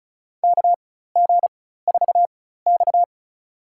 Morse code, heard as a keyed tone, about 36 wpm, spaced as for 16 wpm, 700 Hz. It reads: KG4X
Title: USA Call Signs - 4 Characters - 4x Character Spacing - 35wpm